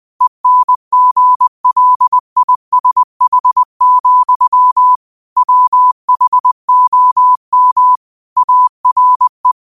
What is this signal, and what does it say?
Morse code, heard as a keyed tone, about 20 wpm, 1000 Hz. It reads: ENGLISH, WHOM ARE
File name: qotdfiles / 20wpm